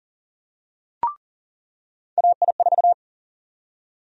Morse code, heard as a keyed tone, about 40 wpm, 700 Hz. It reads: AI4
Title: USA Call Sign Prefixes - 40wpm